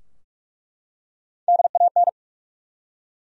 Morse code, heard as a keyed tone, about 46 wpm, 700 Hz. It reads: DAN
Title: USA Common Names - Max 3 Letters - 45wpm